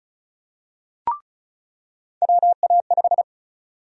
Morse code, heard as a keyed tone, about 34 wpm, 700 Hz. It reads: WA5
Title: USA Call Sign Prefixes - 35wpm